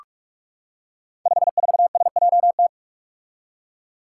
Morse code, heard as a keyed tone, about 44 wpm, 700 Hz. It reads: 54SJT
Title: Five Challenging Letter-Number Combinations 45wpm